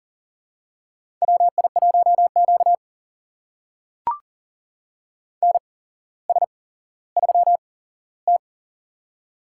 Morse code, heard as a keyed tone, about 40 wpm, spaced as for 17 wpm, 700 Hz. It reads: WI1Q NS3T
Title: USA Call Signs - 4 Characters - 8x Character Spacing - 40wpm